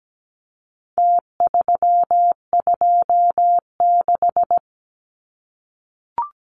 Morse code, heard as a keyed tone, about 17 wpm, 700 Hz. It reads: T326
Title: Numbers - Contest Style - 4-Digit 17wpm